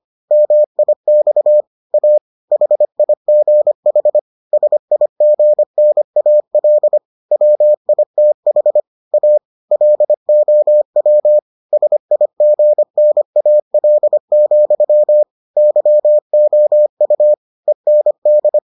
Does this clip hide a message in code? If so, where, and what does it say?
Yes: Morse code from 0.3 to 18.6 s, heard as a keyed tone, about 25 wpm, 600 Hz. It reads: MIX A HIGH SIGNAL WITH A LOW SIGNAL, YOU END